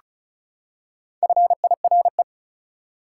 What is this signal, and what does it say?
Morse code, heard as a keyed tone, about 34 wpm, 700 Hz. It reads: FIRE
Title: Four Letter Words - 35wpm